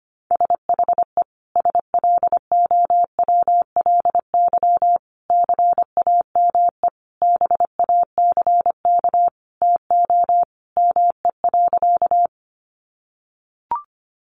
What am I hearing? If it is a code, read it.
Morse code, heard as a keyed tone, about 25 wpm, 700 Hz. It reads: SHE SLOWLY CAME BACK TO ME.
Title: Sentences from Top 1100 Words 25wpm